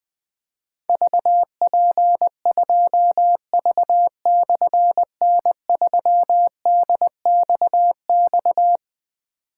Morse code, heard as a keyed tone, about 20 wpm, 700 Hz. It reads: VP2V/N3DXX
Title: Difficult Call Signs 20wpm